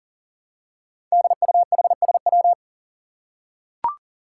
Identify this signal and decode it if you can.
Morse code, heard as a keyed tone, about 40 wpm, 700 Hz. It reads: DUHSW
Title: Five Challenging Letter-Number Combinations 40wpm